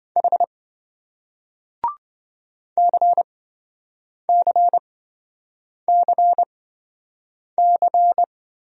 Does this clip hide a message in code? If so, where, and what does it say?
Yes: Morse code from 0.2 to 8.2 s, heard as a keyed tone, about 27 wpm, spaced as for 7 wpm, 700 Hz. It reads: H CCCC